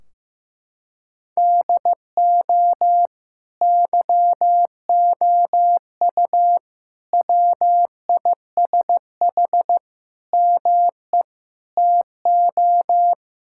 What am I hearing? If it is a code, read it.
Morse code, heard as a keyed tone, about 15 wpm, 700 Hz. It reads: DO YOU WISH ME TO
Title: Sentences from Top 1600 Words 15wpm